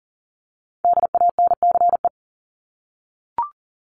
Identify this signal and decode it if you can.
Morse code, heard as a keyed tone, about 40 wpm, 700 Hz. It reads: DANCE
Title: Top 501-750 Words 40wpm